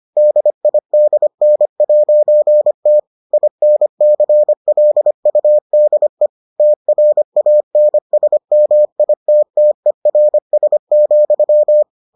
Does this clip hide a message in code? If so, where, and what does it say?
Yes: Morse code from 0.2 to 11.8 s, heard as a keyed tone, about 25 wpm, 600 Hz. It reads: DIDN'T INCLUDE TRANSMITTERS,